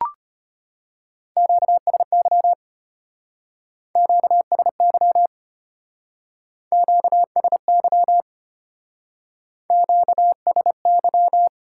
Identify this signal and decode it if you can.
Morse code, heard as a keyed tone, about 30 wpm, 700 Hz. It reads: QSY QSY QSY QSY
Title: Speed-Racing Top 180 QSO Elements 25wpm